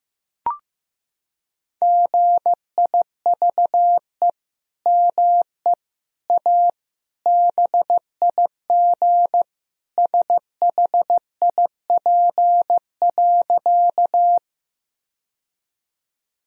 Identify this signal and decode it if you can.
Morse code, heard as a keyed tone, about 16 wpm, 700 Hz. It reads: GIVE ME A BIG SHIP.
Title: Sentences from Top 1000 Words - Review Part 2 - 15wpm